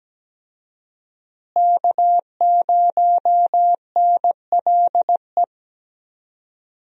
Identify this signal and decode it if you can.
Morse code, heard as a keyed tone, about 17 wpm, 700 Hz. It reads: K0NLE
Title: USA Call Signs - 5 Characters - Repeated 3x - 17wpm